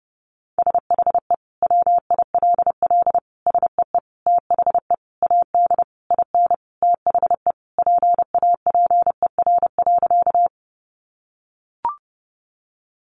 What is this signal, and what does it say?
Morse code, heard as a keyed tone, about 30 wpm, 700 Hz. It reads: SHE WILL SEE THE AD IN THE PAPER.
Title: Sentences from Top 1300 Words 30wpm